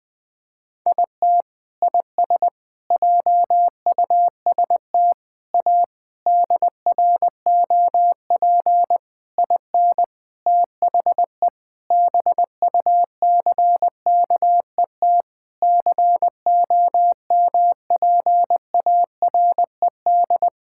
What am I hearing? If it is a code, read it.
Morse code, heard as a keyed tone, about 20 wpm, 700 Hz. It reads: IT IS JUST A DROP IN THE BUCKET COMPARED
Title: Sentences with 8-Word Phrases - 20wpm